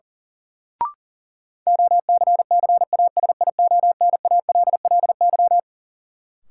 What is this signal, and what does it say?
Morse code, heard as a keyed tone, about 37 wpm, 700 Hz. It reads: OCCASIONALLY
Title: Sentences from Top 1300 Words - New Word Drill Alt - 40wpm